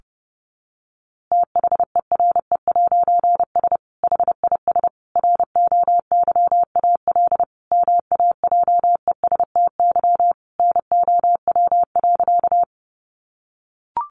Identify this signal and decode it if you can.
Morse code, heard as a keyed tone, about 30 wpm, 700 Hz. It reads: THERE'S HIS ROYAL MAJESTY NOW.
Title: A Sound Of Thunder 30wpm